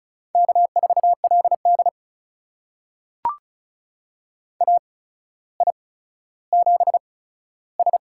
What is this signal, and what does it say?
Morse code, heard as a keyed tone, about 36 wpm, spaced as for 15 wpm, 700 Hz. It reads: K4LD AI7S
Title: USA Call Signs - 4 Characters - 8x Character Spacing - 35wpm